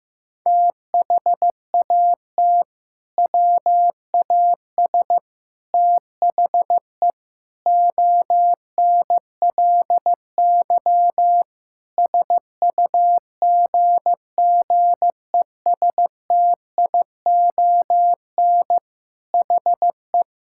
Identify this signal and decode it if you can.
Morse code, heard as a keyed tone, about 15 wpm, 700 Hz. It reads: THAT WAS THE ONLY SUGGESTION HE
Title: Sentences from Top 1600 Words 15wpm